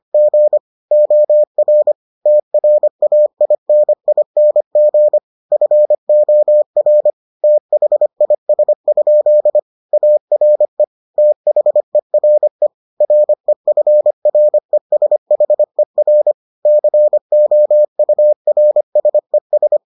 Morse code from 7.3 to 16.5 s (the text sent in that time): THIS? ARE THERE REFRESHER